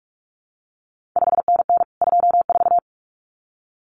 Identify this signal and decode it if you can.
Morse code, heard as a keyed tone, about 46 wpm, 700 Hz. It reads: 5NN 24
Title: CQ WW DX Contest Exchange Encore - 45wpm